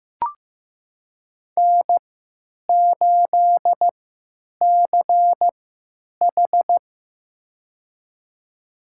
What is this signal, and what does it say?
Morse code, heard as a keyed tone, about 15 wpm, spaced as for 9 wpm, 700 Hz. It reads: N8CH
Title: USA Call Signs - 3x Character Spacing - 15wpm